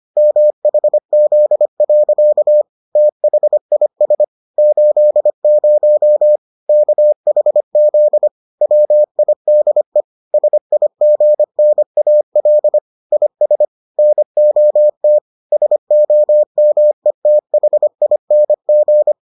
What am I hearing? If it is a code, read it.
Morse code, heard as a keyed tone, about 25 wpm, 600 Hz. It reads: MHZ. THIS 80 KHZ WIDE SIGNAL IS NOT SOMETHING